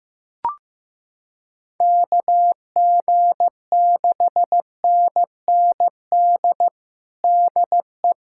Morse code, heard as a keyed tone, about 15 wpm, 700 Hz. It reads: KG6NND DE
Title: USA Callsign DE Callsign - 6 Characters - 15wpm